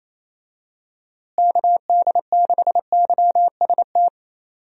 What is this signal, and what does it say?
Morse code, heard as a keyed tone, about 27 wpm, 700 Hz. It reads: KD6YST
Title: USA Call Signs - 6 Characters - Repeated 3x - 28wpm